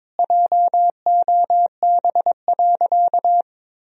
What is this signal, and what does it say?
Morse code, heard as a keyed tone, about 22 wpm, 700 Hz. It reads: JOB.
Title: Sentences from Top 1000 Words - Review Part 3 - 22wpm